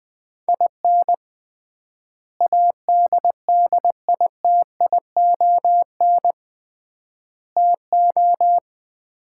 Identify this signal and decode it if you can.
Morse code, heard as a keyed tone, about 21 wpm, 700 Hz. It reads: IN ADDITION TO
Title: 3x Word-Spacing - Sets of 4 Words - 20wpm